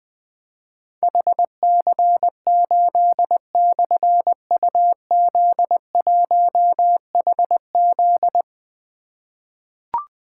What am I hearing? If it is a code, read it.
Morse code, heard as a keyed tone, about 20 wpm, 700 Hz. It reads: HC8/UZ1HZ
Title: Difficult Call Signs 20wpm